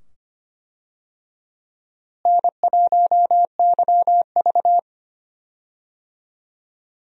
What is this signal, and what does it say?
Morse code, heard as a keyed tone, about 25 wpm, 700 Hz. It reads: N1YV
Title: USA Call Signs - 4 Characters - Repeated 2x - 25wpm